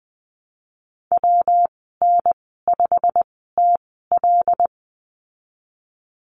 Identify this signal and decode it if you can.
Morse code, heard as a keyed tone, about 20 wpm, spaced as for 14 wpm, 700 Hz. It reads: WN5TL
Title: 128 Lesson 21g - Number 6 - Call Signs 20wpm